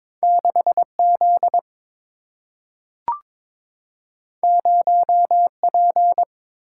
Morse code, heard as a keyed tone, about 22 wpm, 700 Hz. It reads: 6 Z 0 P
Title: Single-Letter Single-Number 22wpm